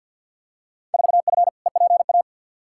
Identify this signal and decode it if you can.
Morse code, heard as a keyed tone, about 46 wpm, 700 Hz. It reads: 4F EPA
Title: Field Day Exchanges Encore - 50wpm